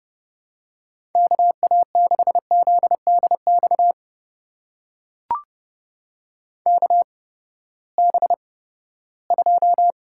Morse code, heard as a keyed tone, about 31 wpm, spaced as for 17 wpm, 700 Hz. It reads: KA6ZDX KB2
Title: USA Call Signs - 8x Character Spacing - 30wpm